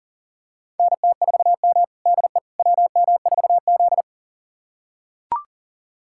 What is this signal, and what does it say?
Morse code, heard as a keyed tone, about 40 wpm, 700 Hz. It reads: NT4M DE WM4Z
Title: USA Callsign DE Callsign - 4 Characters - 40wpm